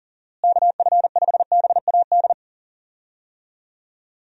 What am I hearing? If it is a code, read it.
Morse code, heard as a keyed tone, about 40 wpm, 700 Hz. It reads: KF5BAD
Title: USA Call Signs - 6 Characters - Repeated 2x - 40wpm